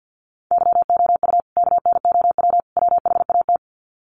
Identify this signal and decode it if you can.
Morse code, heard as a keyed tone, about 48 wpm, 700 Hz. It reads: YOU KNOW WHAT
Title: Sets of 3 Words 50wpm